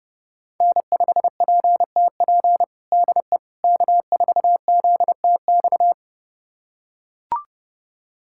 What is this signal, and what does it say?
Morse code, heard as a keyed tone, about 30 wpm, 700 Hz. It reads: N5PTP DE K4ZTX
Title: USA Callsign DE Callsign - 5 Characters - 30wpm